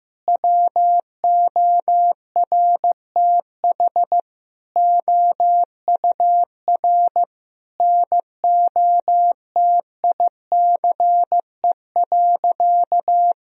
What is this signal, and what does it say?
Morse code, heard as a keyed tone, about 15 wpm, 700 Hz. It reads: WORTH OUR NOTICE.
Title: Sentences from Top 1000 Words - Review Part 2 - 15wpm